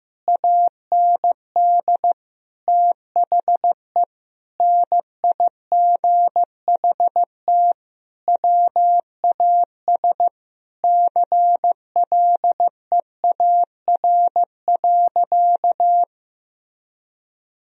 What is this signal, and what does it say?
Morse code, heard as a keyed tone, about 15 wpm, 700 Hz. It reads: AND THE NIGHT WAS CLEAR.
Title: Sentences from Top 1000 Words - Review Part 1 - 15wpm